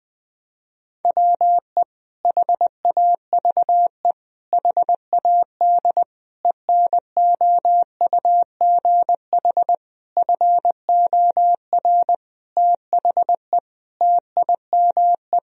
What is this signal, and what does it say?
Morse code, heard as a keyed tone, about 20 wpm, 700 Hz. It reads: WE HAVE HAD ENOUGH FOR THE TIME
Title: Sentences from Top 1000 Words - Review Part 3 - 20wpm